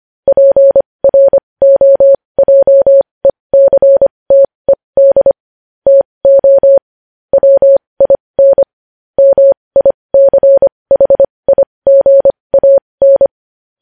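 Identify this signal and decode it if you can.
Morse code, heard as a keyed tone, about 25 wpm, spaced as for 20 wpm, 550 Hz. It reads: PROJECTED TO WIN MICHIGAN
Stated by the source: Morse code news headlines sent at 20 wpm.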